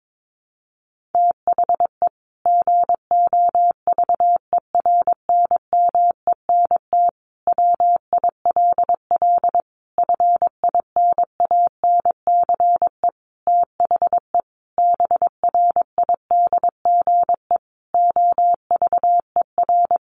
Morse code, heard as a keyed tone, about 22 wpm, 700 Hz. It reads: THE GOVERNMENT WILL FINANCE THE BRIDGE OVER
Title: Sentences from Top 1500 Words 22wpm